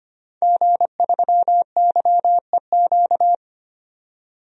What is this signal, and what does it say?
Morse code, heard as a keyed tone, about 25 wpm, 700 Hz. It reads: G3YEQ
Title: DXCC-100 Call Signs - 5 Characters - 25wpm